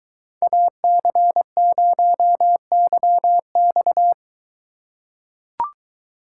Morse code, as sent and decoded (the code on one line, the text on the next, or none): .- -.-. ----- -.-- -..-
AC0YX